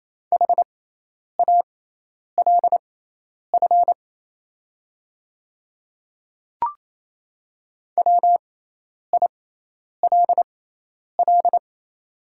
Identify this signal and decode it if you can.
Morse code, heard as a keyed tone, about 28 wpm, spaced as for 10 wpm, 690 Hz. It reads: HALF WILL
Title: 6x Character Spacing - Top 500 Words - 28wpm